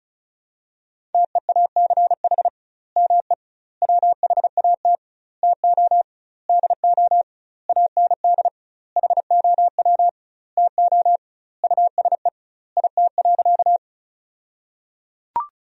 Morse code, heard as a keyed tone, about 36 wpm, 700 Hz. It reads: TEACH ME WHAT TO DO AND HOW TO USE IT.
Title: 2x Word-Spacing - Sentences from Top 500 Words - 35wpm